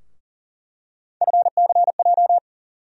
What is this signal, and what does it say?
Morse code, heard as a keyed tone, about 37 wpm, 700 Hz. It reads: FCJ
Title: USA Call Sign Postfixes - 3 Letters 40wpm